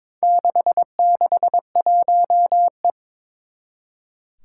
Morse code, heard as a keyed tone, about 22 wpm, 700 Hz. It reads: 661E
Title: Four Challenging Letter-Number Combinations 22wpm